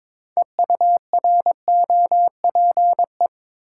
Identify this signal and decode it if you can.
Morse code, heard as a keyed tone, about 22 wpm, 700 Hz. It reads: EUROPE
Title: Sets of 5 Words Encore - New Word Drill - 22wpm